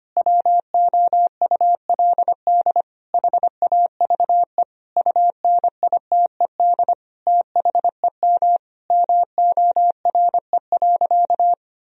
Morse code, heard as a keyed tone, about 25 wpm, 700 Hz. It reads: WOULD HAVE UNITED THEM MORE.
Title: Sentences from Top 1000 Words - Review Part 4 - 25wpm